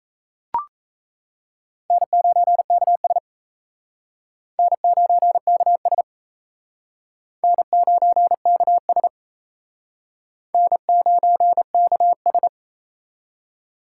Morse code, heard as a keyed tone, about 34 wpm, 700 Hz. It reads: N9KS N9KS N9KS N9KS
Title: Speed Racing - USA 1x2 Call Signs 28wpm